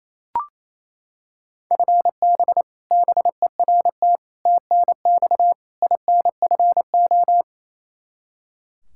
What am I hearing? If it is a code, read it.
Morse code, heard as a keyed tone, about 28 wpm, 700 Hz. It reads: FB BERT TNX INFO
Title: Practice QSO - Exchange 3 - More Info - 28wpm